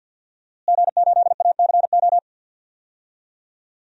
Morse code, heard as a keyed tone, about 52 wpm, 700 Hz. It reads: G8ACO
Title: DXCC-100 Call Signs - 5 Characters - Repeated 3x - 50wpm